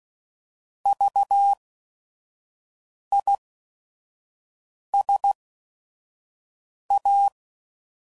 Morse code, heard as a keyed tone, about 16 wpm, spaced as for 5 wpm, 800 Hz. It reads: VISA
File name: Prefixes / By First Letter / Vs